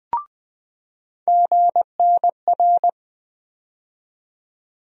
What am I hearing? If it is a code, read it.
Morse code, heard as a keyed tone, about 20 wpm, 700 Hz. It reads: GNR
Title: USA Call Sign Postfixes - 3 Letters 20wpm